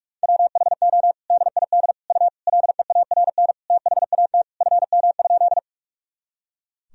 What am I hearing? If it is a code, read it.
Morse code, heard as a keyed tone, about 46 wpm, 700 Hz. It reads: WHO DID U LEARN THAT FM?
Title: Sentences from Top 300 Words and Abbreviations 45wpm